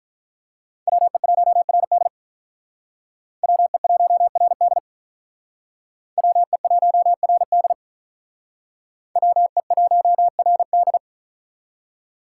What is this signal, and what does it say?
Morse code, heard as a keyed tone, about 45 wpm, 700 Hz. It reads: WE1RD WE1RD WE1RD WE1RD